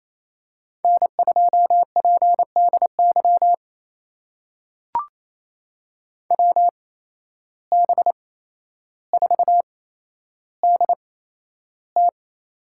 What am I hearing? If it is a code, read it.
Morse code, heard as a keyed tone, about 27 wpm, spaced as for 12 wpm, 700 Hz. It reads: N2PDY WB4DT
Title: USA Call Signs - 5 Characters - 8x Character Spacing - 28wpm